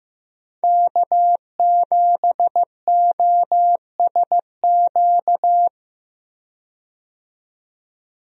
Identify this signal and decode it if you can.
Morse code, heard as a keyed tone, about 15 wpm, 700 Hz. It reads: K7OSQ
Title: USA Call Signs - 5 Characters - 15wpm